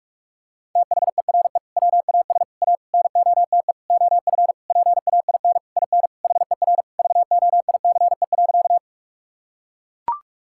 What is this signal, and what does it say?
Morse code, heard as a keyed tone, about 43 wpm, 700 Hz. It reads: THERE WAS A NOTE OF PAIN IN HER VOICE.
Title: Sentences from Top 1600 Words 45wpm